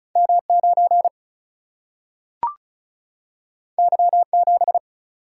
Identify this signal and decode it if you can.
Morse code, heard as a keyed tone, about 36 wpm, 700 Hz. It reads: M 9 Y 7
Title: Single-Letter Single-Number 35wpm